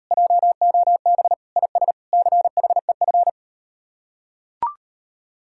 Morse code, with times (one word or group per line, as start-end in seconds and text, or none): 0.1-1.3 JOB
1.6-1.9 IS
2.1-3.3 CHEF